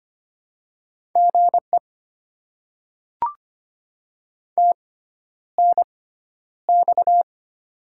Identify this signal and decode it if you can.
Morse code, heard as a keyed tone, about 25 wpm, spaced as for 11 wpm, 690 Hz. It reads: GE TNX
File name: Top 20 QSO Elements - 6x Character Spacing - 25wpm